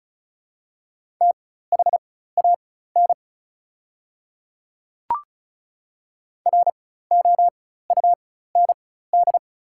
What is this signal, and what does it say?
Morse code, heard as a keyed tone, about 36 wpm, spaced as for 16 wpm, 690 Hz. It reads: THAN ROUND
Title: 4x Character Spacing - Top 200 Words - 35wpm